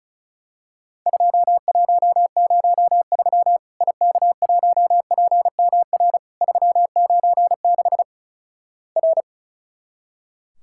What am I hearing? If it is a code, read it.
Morse code, heard as a keyed tone, about 36 wpm, 700 Hz. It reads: 2103 IK1PMR 396 R
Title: Worked All Europe DX Contest - QTC Exchange - 35wpm